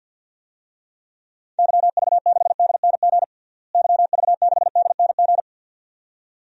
Morse code, heard as a keyed tone, about 47 wpm, 700 Hz. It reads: YV6DNG YV6DNG
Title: DXCC-100 Call Signs - 6 Characters - Repeated 2x - 50wpm